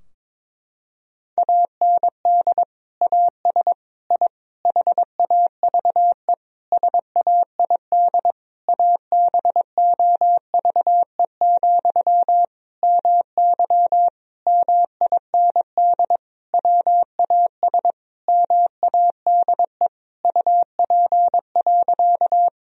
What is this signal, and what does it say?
Morse code, heard as a keyed tone, about 22 wpm, 700 Hz. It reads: AND AS I HAVE SAID ABOVE, MY MIND WAS MADE UP.
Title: Sentences from Top 1000 Words - Review Part 2 - 22wpm